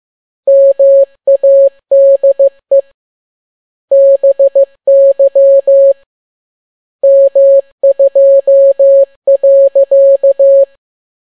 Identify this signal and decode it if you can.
Morse code, heard as a keyed tone, about 15 wpm, 550 Hz. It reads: MADE BY M2.